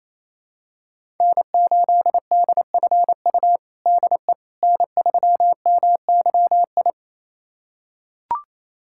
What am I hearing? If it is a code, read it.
Morse code, heard as a keyed tone, about 28 wpm, 700 Hz. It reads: N8DFU DE N3MYI
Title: USA Callsign DE Callsign - 5 Characters - 28wpm